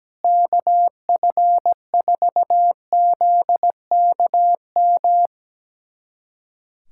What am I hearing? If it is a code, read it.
Morse code, heard as a keyed tone, about 17 wpm, 700 Hz. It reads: KF4ZKM
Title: USA Callsign DE Callsign 17wpm